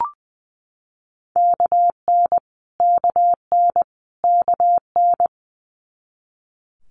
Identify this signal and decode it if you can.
Morse code, heard as a keyed tone, about 20 wpm, 700 Hz. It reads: KN KN KN
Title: Multi-letter Phonemes Repeated 3x 20wpm